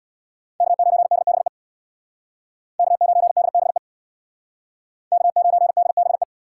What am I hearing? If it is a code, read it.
Morse code, heard as a keyed tone, about 62 wpm, 700 Hz. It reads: K9DBE K9DBE K9DBE
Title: Speed Racing - USA 1x3 Call Signs 50wpm